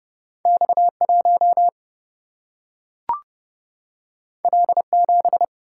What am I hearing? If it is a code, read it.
Morse code, heard as a keyed tone, about 32 wpm, 700 Hz. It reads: X 1 L 7
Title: Single-Letter Single-Number 30wpm